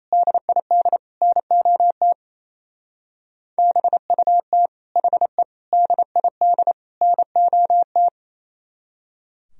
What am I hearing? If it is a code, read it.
Morse code, heard as a keyed tone, about 33 wpm, 700 Hz. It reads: DID NOT BUT HE DID NOT
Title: Speed-Racing - Sets of 4 Words 28wpm